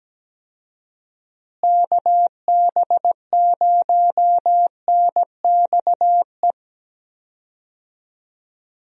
Morse code, heard as a keyed tone, about 17 wpm, 700 Hz. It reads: KB0NXE
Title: USA Call Signs - 6 Characters - Repeated 3x - 17wpm